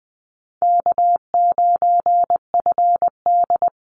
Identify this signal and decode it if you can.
Morse code, heard as a keyed tone, about 20 wpm, 700 Hz. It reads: K9FD
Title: Call Signs Repeated 3x - 20wpm